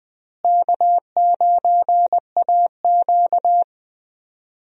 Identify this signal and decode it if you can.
Morse code, heard as a keyed tone, about 20 wpm, 700 Hz. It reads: K9AQ
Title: USA Call Signs - 4 Characters - Repeated 2x - 20wpm